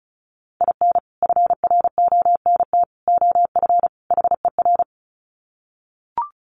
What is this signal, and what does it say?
Morse code, heard as a keyed tone, about 34 wpm, 700 Hz. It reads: IN FRONT OF HER
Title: Sets of 4 Words 35wpm